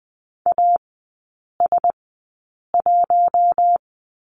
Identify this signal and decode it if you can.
Morse code, heard as a keyed tone, about 20 wpm, spaced as for 8 wpm, 700 Hz. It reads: AS1
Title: 041 Lesson 8e - Number 1 - Three Characters 20wpm